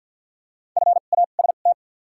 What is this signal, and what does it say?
Morse code, heard as a keyed tone, about 47 wpm, spaced as for 35 wpm, 700 Hz. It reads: FAST
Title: 2x Character Spacing - Top 500 Words - 50wpm